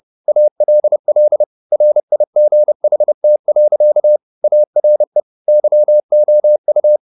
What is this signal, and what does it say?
Morse code, heard as a keyed tone, about 30 wpm, 600 Hz. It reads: ALL RIGHT. ARE YOU
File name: the old man and the sea / the old man and the sea 1 30wpm